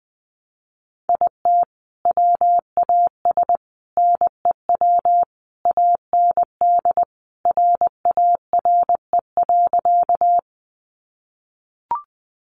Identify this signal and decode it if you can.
Morse code, heard as a keyed tone, about 19 wpm, 700 Hz. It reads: IT WAS NEW AND RARE.
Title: Sentences from Top 1100 Words 20wpm